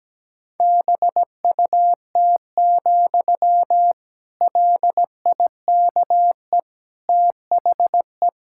Morse code, heard as a keyed tone, about 17 wpm, 700 Hz. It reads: BUT, LIKE THE
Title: Sentences from Top 1600 Words 17wpm